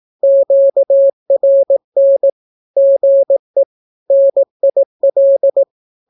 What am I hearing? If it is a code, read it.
Morse code, heard as a keyed tone, about 18 wpm, 550 Hz. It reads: QRN GE NIL
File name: naukapodstaw / skroty / skroty